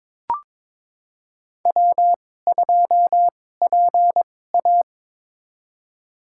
Morse code, heard as a keyed tone, about 22 wpm, spaced as for 16 wpm, 700 Hz. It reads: W2PA
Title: USA Call Signs - 4 Characters - 2x Character Spacing - 22wpm